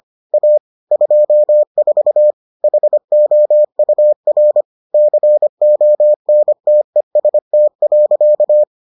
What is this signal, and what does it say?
Morse code, heard as a keyed tone, about 25 wpm, 600 Hz. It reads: A 24 HOUR CONTEST.